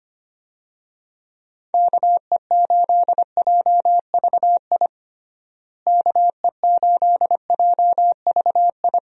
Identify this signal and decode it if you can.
Morse code, heard as a keyed tone, about 25 wpm, 700 Hz. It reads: KE8JVI KE8JVI
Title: USA Call Signs - 6 Characters - Repeated 2x - 25wpm